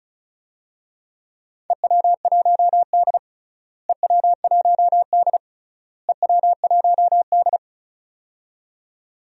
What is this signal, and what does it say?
Morse code, heard as a keyed tone, about 34 wpm, 700 Hz. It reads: EW1D EW1D EW1D
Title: DXCC-100 Call Signs - 4 Characters - Repeated 3x - 35wpm